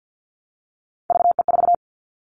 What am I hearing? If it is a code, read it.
Morse code, heard as a keyed tone, about 48 wpm, 700 Hz. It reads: VE4
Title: Callsign Prefixes for Top 25 DXCC - Repeated 3x - 50wpm